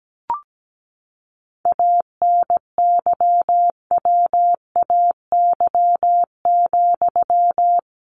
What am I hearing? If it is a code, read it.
Morse code, heard as a keyed tone, about 17 wpm, 700 Hz. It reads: ANYWAY,
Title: A Sound Of Thunder 17wpm